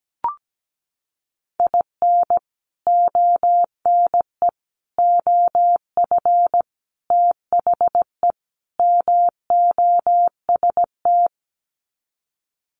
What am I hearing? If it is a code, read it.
Morse code, heard as a keyed tone, about 17 wpm, 700 Hz. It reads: IN ONE OF THE MOST
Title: Sets of 5 Words Encore 17wpm